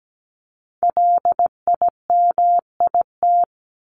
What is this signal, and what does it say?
Morse code, heard as a keyed tone, about 17 wpm, 700 Hz. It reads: LIMIT
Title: ED Suffix - Top 100 Words with Base Words 17wpm